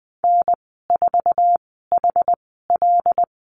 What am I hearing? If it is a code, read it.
Morse code, heard as a keyed tone, about 19 wpm, spaced as for 14 wpm, 700 Hz. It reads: N4HL
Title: 088 Lesson 15g - Number 5 - Call Signs 20wpm